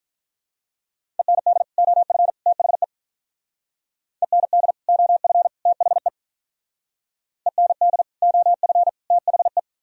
Morse code, heard as a keyed tone, about 47 wpm, 700 Hz. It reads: END OF THE END OF THE END OF THE